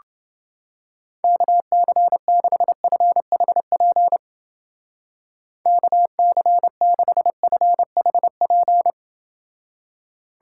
Morse code, heard as a keyed tone, about 29 wpm, 700 Hz. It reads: KC6FHP KC6FHP